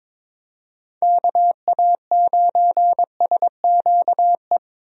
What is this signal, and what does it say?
Morse code, heard as a keyed tone, about 21 wpm, 700 Hz. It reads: KA9SQE
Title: USA Call Signs - 6 Characters - Repeated 2x - 22wpm